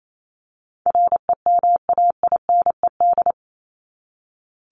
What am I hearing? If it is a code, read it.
Morse code, heard as a keyed tone, about 28 wpm, 700 Hz. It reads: REMAINED
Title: ED Suffix - Top 100 Words with Base Words 28wpm